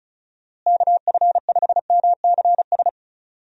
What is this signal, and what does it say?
Morse code, heard as a keyed tone, about 34 wpm, 700 Hz. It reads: KF5MCS
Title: USA Call Signs - 6 Characters - Repeated 2x - 35wpm